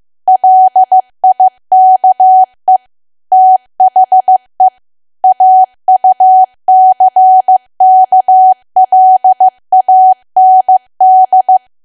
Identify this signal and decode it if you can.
Morse code, heard as a keyed tone, about 15 wpm, 750 Hz. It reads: LIKE THE AUCKLAND
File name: cwa / W1AW-Level2 / 15WPM